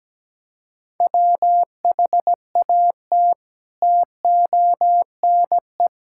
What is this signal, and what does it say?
Morse code, heard as a keyed tone, about 17 wpm, 700 Hz. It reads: WHAT TONE
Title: Sentences from Top 1600 Words 17wpm